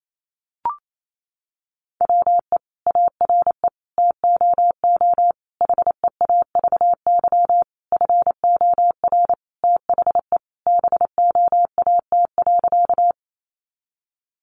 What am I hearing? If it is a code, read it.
Morse code, heard as a keyed tone, about 28 wpm, 700 Hz. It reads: WE ARE TOO HEAVY FOR THE BOAT.
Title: Sentences from Top 1100 Words 28wpm